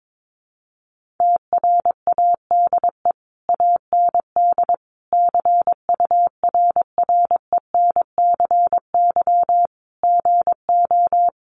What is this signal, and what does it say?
Morse code, heard as a keyed tone, about 22 wpm, 700 Hz. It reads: TRADE AND CURRENCY GO
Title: Sentences from Top 1500 Words 22wpm